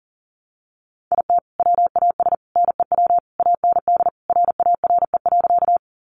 Morse code, heard as a keyed tone, about 40 wpm, 700 Hz. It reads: IT WAS NEW AND RARE.
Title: Sentences from Top 1100 Words 40wpm